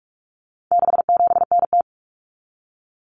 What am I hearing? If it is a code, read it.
Morse code, heard as a keyed tone, about 44 wpm, 700 Hz. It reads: 67NT
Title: Common Cut Numbers - 4-Digit 45wpm